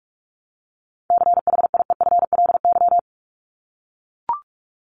Morse code, heard as a keyed tone, about 44 wpm, 700 Hz. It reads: CHIEFLY
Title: LY Suffix - Top 100 Words with Base Words 45wpm